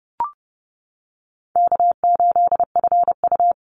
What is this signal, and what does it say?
Morse code, heard as a keyed tone, about 30 wpm, 700 Hz. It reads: K8FU
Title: Easy Call Signs - 30wpm